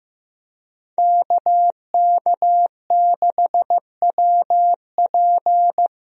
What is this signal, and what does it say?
Morse code, heard as a keyed tone, about 15 wpm, 700 Hz. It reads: KK6WP
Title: USA Call Signs - 5 Characters - Repeated 3x - 15wpm